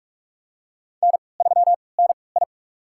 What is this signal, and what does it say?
Morse code, heard as a keyed tone, about 46 wpm, spaced as for 26 wpm, 690 Hz. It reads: N3NI
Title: USA Call Signs - 4 Characters - 3x Character Spacing - 45wpm